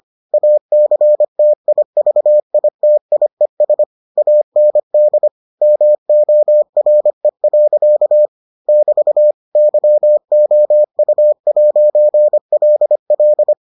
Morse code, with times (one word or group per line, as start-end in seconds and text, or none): 0.3-3.8 ACTIVITIES
4.2-5.3 AND
5.6-8.3 MORE.
8.7-9.3 <BT>
9.6-13.5 YOU'LL